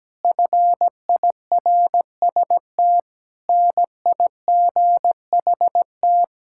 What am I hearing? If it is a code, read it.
Morse code, heard as a keyed tone, about 17 wpm, 700 Hz. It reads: FIRST NIGHT
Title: Sentences from Top 1000 Words - Review Part 5 - 17wpm